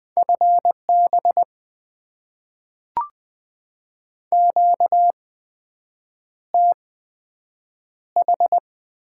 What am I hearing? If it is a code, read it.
Morse code, heard as a keyed tone, about 20 wpm, spaced as for 7 wpm, 700 Hz. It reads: FB QTH
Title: Top 20 QSO Elements - 8x Character Spacing - 20wpm